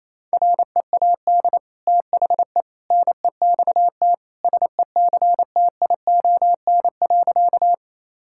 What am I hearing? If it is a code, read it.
Morse code, heard as a keyed tone, about 28 wpm, 700 Hz. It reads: READ THE NEXT SECTION.
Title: Sentences from Top 1000 Words - Review Part 5 - 28wpm